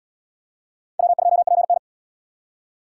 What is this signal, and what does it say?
Morse code, heard as a keyed tone, about 77 wpm, 700 Hz. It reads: W2PN